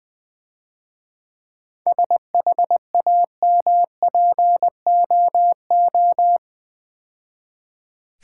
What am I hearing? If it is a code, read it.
Morse code, heard as a keyed tone, about 22 wpm, 700 Hz. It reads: SHAMPOO
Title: Easy Words - Speed-Racing with Hints - 20wpm